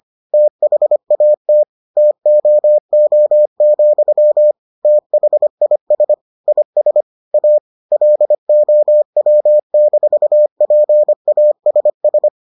Morse code, heard as a keyed tone, about 25 wpm, 600 Hz. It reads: THAT TOO, THIS IS A LOW-PASS